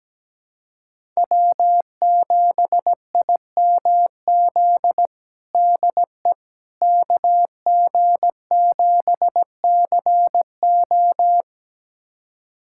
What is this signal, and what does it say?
Morse code, heard as a keyed tone, about 17 wpm, 700 Hz. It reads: W7IMZ DE KG7CO
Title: USA Callsign DE Callsign - 5 Characters - 17wpm